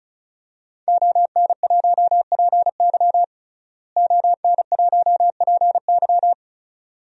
Morse code, heard as a keyed tone, about 34 wpm, 700 Hz. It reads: ON1PY ON1PY
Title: DXCC-100 Call Signs - 5 Characters - Repeated 2x - 35wpm